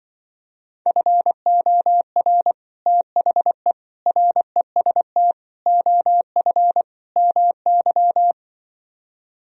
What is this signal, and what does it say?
Morse code, heard as a keyed tone, about 25 wpm, 700 Hz. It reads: FOR THE REST OF MY
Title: Speed-Racing - Sets of 5 Words 20wpm